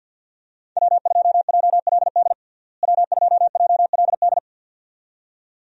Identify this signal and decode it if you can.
Morse code, heard as a keyed tone, about 52 wpm, 700 Hz. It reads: W2JLD W2JLD
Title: USA Call Signs - 5 Characters - Repeated 2x - 50wpm